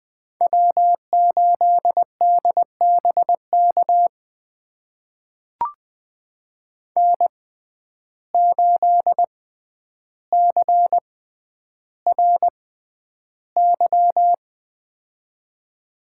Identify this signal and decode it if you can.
Morse code, heard as a keyed tone, about 20 wpm, spaced as for 10 wpm, 700 Hz. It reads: W8DBK N8CRY
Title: USA Call Signs - 6x Character Spacing - 20wpm